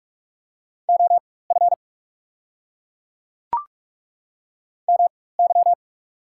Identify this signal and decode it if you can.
Morse code, heard as a keyed tone, about 42 wpm, spaced as for 20 wpm, 690 Hz. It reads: OF MY
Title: 4x Character Spacing - Top 300 Words - 45wpm